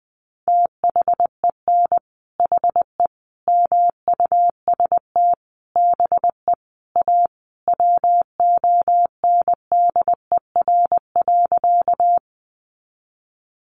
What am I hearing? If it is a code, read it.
Morse code, heard as a keyed tone, about 20 wpm, 700 Hz. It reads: THEN HE MUST BE A WONDER.
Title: Sentences from Top 500 Words 20wpm